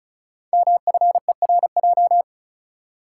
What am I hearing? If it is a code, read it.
Morse code, heard as a keyed tone, about 35 wpm, 700 Hz. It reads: MFERJ
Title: Code Groups - 5 Letters - 35wpm